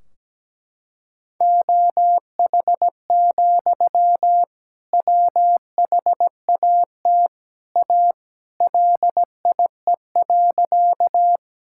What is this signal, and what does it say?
Morse code, heard as a keyed tone, about 17 wpm, 700 Hz. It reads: OH, WHAT A LIE.
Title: Sentences from Top 1600 Words 17wpm